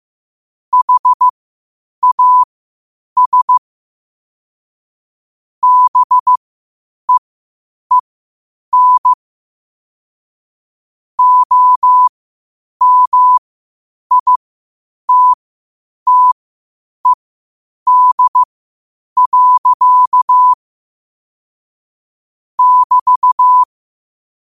Morse code, heard as a keyed tone, about 15 wpm, spaced as for 8 wpm, 1000 Hz. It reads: HAS BEEN OMITTED. <BT>
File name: qotdfiles / 8wpm